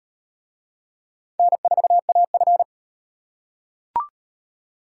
Morse code, heard as a keyed tone, about 35 wpm, 700 Hz. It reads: N4AF